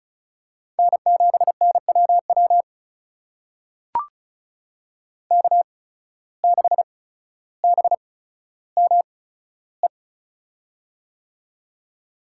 Morse code, heard as a keyed tone, about 36 wpm, spaced as for 15 wpm, 700 Hz. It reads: N7NWW K6BME
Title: USA Call Signs - 5 Characters - 8x Character Spacing - 35wpm